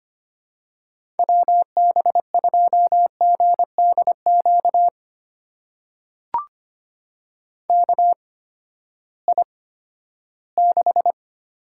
Morse code, heard as a keyed tone, about 25 wpm, spaced as for 14 wpm, 700 Hz. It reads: WB2GDQ KI6
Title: USA Call Signs - 6 Characters - 8x Character Spacing - 25wpm